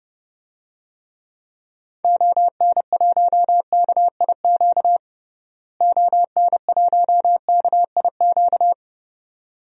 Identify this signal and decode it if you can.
Morse code, heard as a keyed tone, about 32 wpm, 700 Hz. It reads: ON1KIQ ON1KIQ
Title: DXCC-100 Call Signs - 6 Characters - Repeated 2x - 30wpm